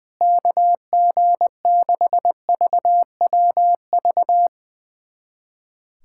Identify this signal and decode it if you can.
Morse code, heard as a keyed tone, about 20 wpm, 700 Hz. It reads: KG6VWV
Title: USA Callsign DE Callsign - 6 Characters - 20wpm